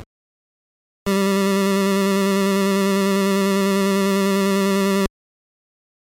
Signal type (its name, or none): none